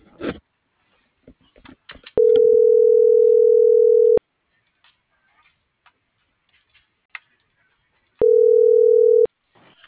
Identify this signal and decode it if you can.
Ringback tone (North America), 440 + 480 Hz.